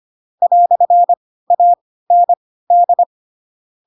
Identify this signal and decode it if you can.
Morse code, heard as a keyed tone, about 25 wpm, spaced as for 16 wpm, 700 Hz. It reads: "AND
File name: Pinocchio / Pinocchio ch4 10 25 ik1hge